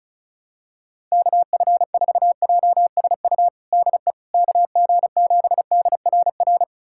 Morse code, heard as a keyed tone, about 34 wpm, 700 Hz. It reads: KF4JSU DE KG7DRR
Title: USA Callsign DE Callsign 35wpm